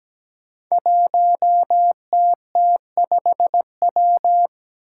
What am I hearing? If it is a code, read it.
Morse code, heard as a keyed tone, about 17 wpm, 700 Hz. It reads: 1TT5W
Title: Five Challenging Letter-Number Combinations 17wpm